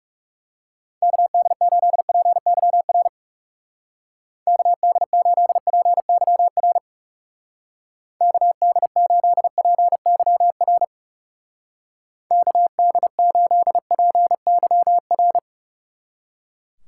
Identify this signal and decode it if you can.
Morse code, heard as a keyed tone, about 38 wpm, 700 Hz. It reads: KD8PYR KD8PYR KD8PYR KD8PYR
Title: Speed Racing - USA 2x3 Call Signs 30wpm